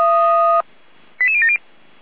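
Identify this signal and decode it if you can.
Five-tone selective call (ZVEI), heard as quick runs of five tones, tones of about 70 ms. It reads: 88070
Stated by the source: ZVEI five-tone call 88199 received by SDR, from the Feuerwehr Orlamünde audio server.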